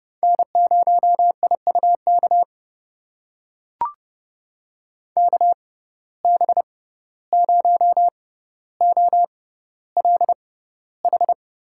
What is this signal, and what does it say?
Morse code, heard as a keyed tone, about 30 wpm, spaced as for 15 wpm, 700 Hz. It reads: N0IUK KB0OLH
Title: USA Call Signs - 6x Character Spacing - 30wpm